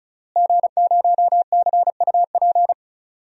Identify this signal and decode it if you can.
Morse code, heard as a keyed tone, about 36 wpm, 700 Hz. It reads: G0CUP
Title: DXCC-100 Call Signs - 5 Characters - Repeated 3x - 35wpm